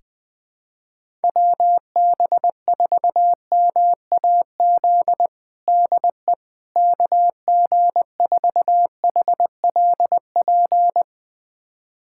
Morse code, heard as a keyed tone, about 20 wpm, 700 Hz. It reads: WB4MAZ DE KG4HLP
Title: USA Callsign DE Callsign - 6 Characters - 20wpm